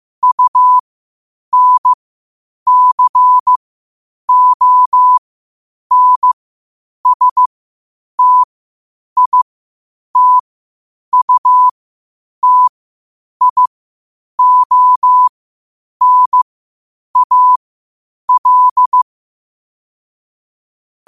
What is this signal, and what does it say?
Morse code, heard as a keyed tone, about 15 wpm, spaced as for 8 wpm, 1000 Hz. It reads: UNCONSTITUTIONAL
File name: qotdfiles / 8wpm